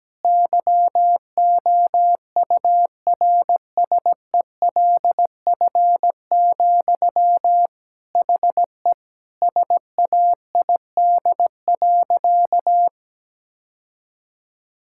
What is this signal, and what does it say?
Morse code, heard as a keyed tone, about 17 wpm, 700 Hz. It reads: YOURSELF, HE SAID.
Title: Sentences from Top 1600 Words 17wpm